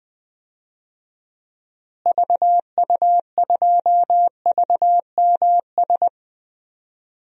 Morse code, heard as a keyed tone, about 20 wpm, 700 Hz. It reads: VU2VMS
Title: DXCC-100 Call Signs - 6 Characters - Repeated 2x - 20wpm